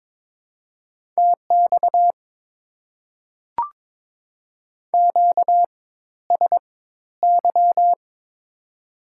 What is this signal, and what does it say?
Morse code, heard as a keyed tone, about 22 wpm, spaced as for 13 wpm, 700 Hz. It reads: TX QSY
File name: Top 90 QSO Elements - 4x Character Spacing - 22wpm